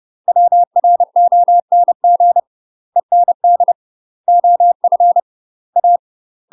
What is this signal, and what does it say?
Morse code, heard as a keyed tone, about 30 wpm, 700 Hz. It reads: WRONG END OF A
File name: UK MASTER QSOs TRAINING / 30WPM / UK MASTER QSOs 0019